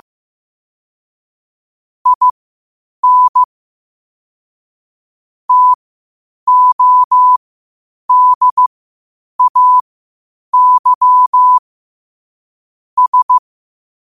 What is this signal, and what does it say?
Morse code, heard as a keyed tone, about 15 wpm, spaced as for 8 wpm, 1000 Hz. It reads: IN TODAY S
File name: qotdfiles / 8wpm